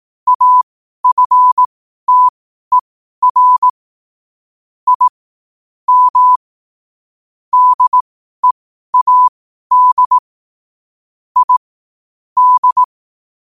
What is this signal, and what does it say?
Morse code, heard as a keyed tone, about 18 wpm, spaced as for 12 wpm, 1000 Hz. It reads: AFTER I M DEAD I D
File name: qotdfiles / 12wpm